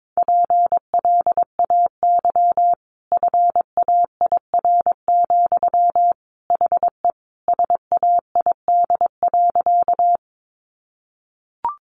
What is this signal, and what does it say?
Morse code, heard as a keyed tone, about 22 wpm, 700 Hz. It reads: PLAY FAIR, HE SAID.
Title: Sentences from Top 1300 Words 22wpm